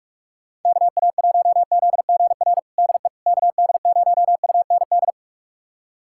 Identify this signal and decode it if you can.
Morse code, heard as a keyed tone, about 42 wpm, 700 Hz. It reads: KA1ZGR DE KD0UND
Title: USA Callsign DE Callsign - 6 Characters - 45wpm